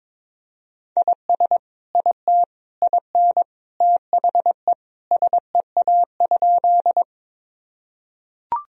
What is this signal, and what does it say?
Morse code, heard as a keyed tone, about 22 wpm, 700 Hz. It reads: IS IT IN THE SEA?
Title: Sentences from Top 1000 Words - Review Part 1 - 22wpm